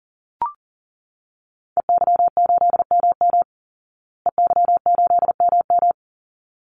Morse code, heard as a keyed tone, about 40 wpm, 700 Hz. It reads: EY8MM EY8MM
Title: Call Signs Repeated 3x - 40wpm